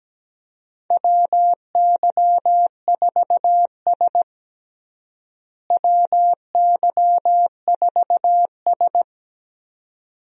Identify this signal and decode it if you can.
Morse code, heard as a keyed tone, about 17 wpm, 700 Hz. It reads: WY4S WY4S
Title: USA Call Signs - 4 Characters - Repeated 2x - 17wpm